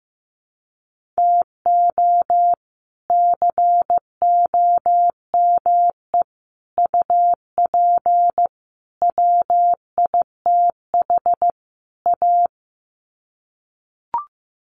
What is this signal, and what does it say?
Morse code, heard as a keyed tone, about 15 wpm, 700 Hz. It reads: TO COME UP WITH A
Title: Sets of 5 Words 15wpm